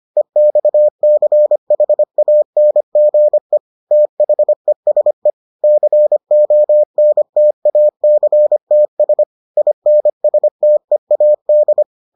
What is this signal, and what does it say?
Morse code, heard as a keyed tone, about 25 wpm, 600 Hz. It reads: EXCHANGE THESE CONTACTS INSTEAD